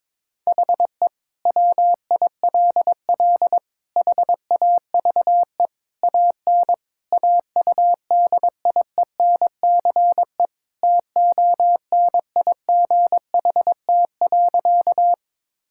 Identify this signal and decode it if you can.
Morse code, heard as a keyed tone, about 22 wpm, 700 Hz. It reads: HE WILL HAVE AN AUDIENCE TONIGHT.
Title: Sentences from Top 1000 Words - Review Part 5 - 22wpm